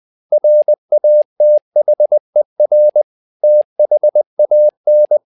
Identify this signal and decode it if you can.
Morse code, heard as a keyed tone, about 20 wpm, 600 Hz. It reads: RATHER THAN
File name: tower trouble 20wpm